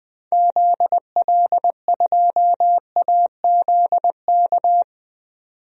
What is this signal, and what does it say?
Morse code, heard as a keyed tone, about 20 wpm, 700 Hz. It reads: ZL2AZK
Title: DXCC-100 Call Signs - 6 Characters - 20wpm